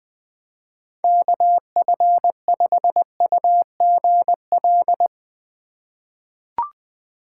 Morse code, heard as a keyed tone, about 20 wpm, 700 Hz. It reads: KF5UGL
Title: USA Call Signs - 2x Character Spacing - 20wpm